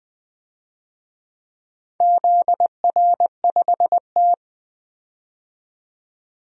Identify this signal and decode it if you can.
Morse code, heard as a keyed tone, about 20 wpm, 700 Hz. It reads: ZR5T
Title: DXCC-100 Call Signs - 4 Characters - 20wpm